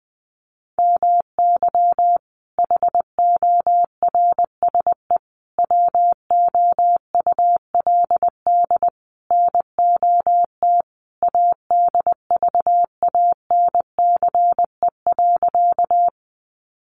Morse code, heard as a keyed tone, about 20 wpm, 700 Hz. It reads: MY HORSE WOULD NOT ADVANCE.
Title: Sentences from Top 1400 Words 20wpm